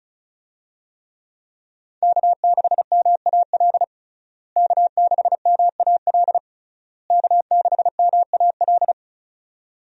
Morse code, heard as a keyed tone, about 36 wpm, 700 Hz. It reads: K6MAL K6MAL K6MAL
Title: USA Call Signs - 5 Characters - Repeated 3x - 35wpm